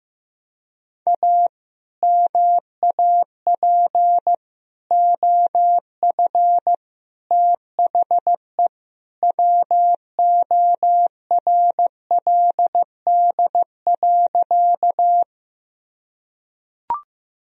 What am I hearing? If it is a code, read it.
Morse code, heard as a keyed tone, about 15 wpm, 700 Hz. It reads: A MAP OF THE WORLD.
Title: Sentences from Top 1000 Words - Review Part 2 - 15wpm